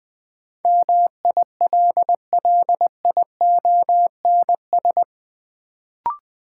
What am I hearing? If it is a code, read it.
Morse code, heard as a keyed tone, about 20 wpm, 700 Hz. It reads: MILLIONS
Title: Sets of 5 Words - New Word Drill - 20wpm